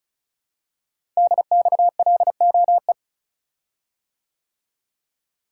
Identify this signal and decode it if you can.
Morse code, heard as a keyed tone, about 36 wpm, 690 Hz. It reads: DXLOE
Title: Code Groups - 5 Letters - 35wpm